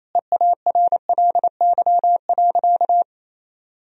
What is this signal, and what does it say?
Morse code, heard as a keyed tone, about 28 wpm, 700 Hz. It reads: EARLY.
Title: Sentences from Top 1000 Words - Review Part 4 - 28wpm